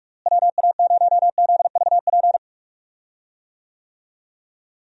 Morse code, heard as a keyed tone, about 46 wpm, 700 Hz. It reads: WA0ZVP
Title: USA Call Signs - 6 Characters - Repeated 2x - 45wpm